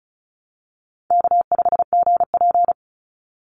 Morse code, heard as a keyed tone, about 34 wpm, 700 Hz. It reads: K5GP
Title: Call Signs Repeated 2x - 35wpm